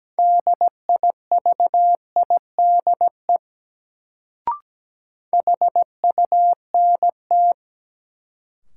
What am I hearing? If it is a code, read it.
Morse code, heard as a keyed tone, about 17 wpm, 700 Hz. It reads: DIVIDE HUNT
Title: Sentences from Top 1600 Words - New Word Drill - 17wpm